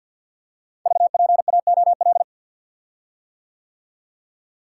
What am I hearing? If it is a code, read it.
Morse code, heard as a keyed tone, about 50 wpm, 700 Hz. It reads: VPAOL
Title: Code Groups - 5 Letters - 50wpm